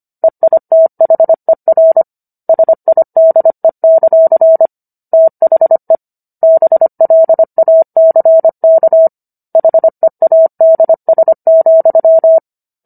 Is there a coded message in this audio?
Yes: Morse code from 0.2 to 12.4 s, heard as a keyed tone, about 25 wpm, 650 Hz. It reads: EITHER SIDE; THE BLACK HEADS,